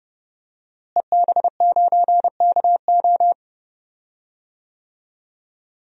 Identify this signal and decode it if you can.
Morse code, heard as a keyed tone, about 31 wpm, 700 Hz. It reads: EB9KO
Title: DXCC-100 Call Signs - 5 Characters - Repeated 3x - 30wpm